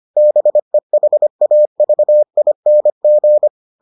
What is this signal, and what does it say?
Morse code, heard as a keyed tone, about 25 wpm, 600 Hz. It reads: BEHAVING